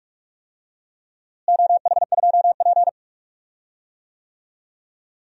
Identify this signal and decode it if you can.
Morse code, heard as a keyed tone, about 47 wpm, 700 Hz. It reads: OH2P
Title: DXCC-100 Call Signs - 4 Characters - Repeated 3x - 45wpm